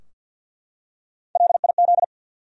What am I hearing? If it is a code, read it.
Morse code, heard as a keyed tone, about 52 wpm, 700 Hz. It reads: LIZ
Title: USA Common Names - 50wpm